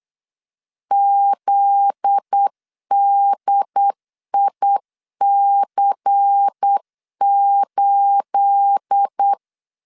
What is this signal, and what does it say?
Morse code, heard as a keyed tone, about 8 wpm, 800 Hz. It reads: ZDIC8
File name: LEZIONI / 10 SIGNS / 10C DOT 8wpm 3649